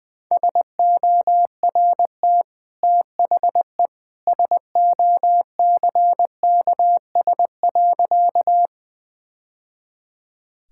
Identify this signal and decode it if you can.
Morse code, heard as a keyed tone, about 20 wpm, 700 Hz. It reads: SORT THE SOCKS.
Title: Easy Sentences with Hints - 20wpm